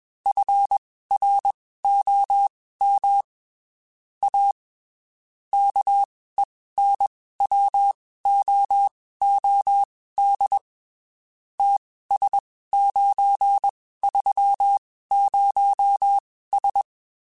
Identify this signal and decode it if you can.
Morse code, heard as a keyed tone, about 21 wpm, spaced as for 15 wpm, 800 Hz. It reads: FROM A KENWOOD TS930S